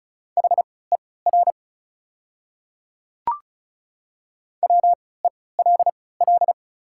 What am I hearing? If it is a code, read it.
Morse code, heard as a keyed tone, about 36 wpm, spaced as for 20 wpm, 690 Hz. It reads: HER WELL
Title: 3x Character Spacing - Top 200 Words - 35wpm